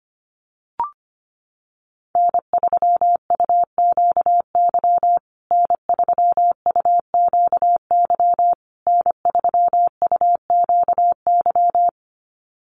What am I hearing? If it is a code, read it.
Morse code, heard as a keyed tone, about 25 wpm, 700 Hz. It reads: N3UQY N3UQY N3UQY
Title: USA Call Signs Repeated 3x 25wpm